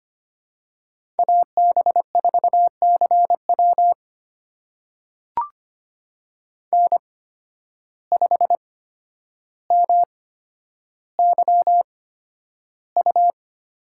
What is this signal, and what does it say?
Morse code, heard as a keyed tone, about 25 wpm, spaced as for 11 wpm, 700 Hz. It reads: AB4CW N5MYU
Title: USA Call Signs - 8x Character Spacing - 25wpm